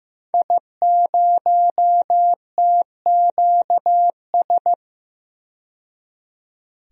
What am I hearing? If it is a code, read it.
Morse code, heard as a keyed tone, about 15 wpm, 700 Hz. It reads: I0TQS
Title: DXCC-100 Call Signs - 5 Characters - 15wpm